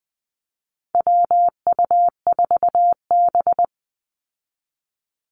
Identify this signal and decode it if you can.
Morse code, heard as a keyed tone, about 20 wpm, 700 Hz. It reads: WU4B
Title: Call Signs Repeated 2x - 20wpm